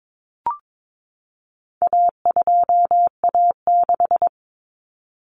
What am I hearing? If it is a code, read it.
Morse code, heard as a keyed tone, about 22 wpm, 700 Hz. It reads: A2A6
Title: Common Cut Numbers - 4-Digit 22wpm